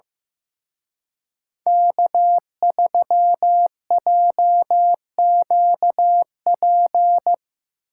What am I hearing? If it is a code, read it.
Morse code, heard as a keyed tone, about 17 wpm, 700 Hz. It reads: K3JQP
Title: Speed Racing - USA 1x3 Call Signs 15wpm